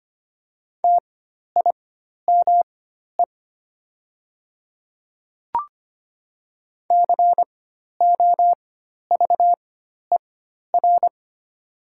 Morse code, heard as a keyed tone, about 25 wpm, spaced as for 12 wpm, 700 Hz. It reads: TIME COVER
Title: 4x Character Spacing - Top 500 Words - 25wpm